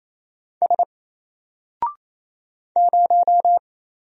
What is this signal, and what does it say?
Morse code, heard as a keyed tone, about 27 wpm, 700 Hz. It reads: S 0
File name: Single-Letter-Number-28wpm